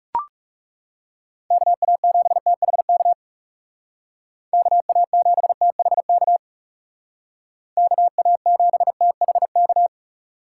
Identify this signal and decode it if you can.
Morse code, heard as a keyed tone, about 38 wpm, 700 Hz. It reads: KA7THK KA7THK KA7THK
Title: Speed Racing - USA 2x3 Call Signs 30wpm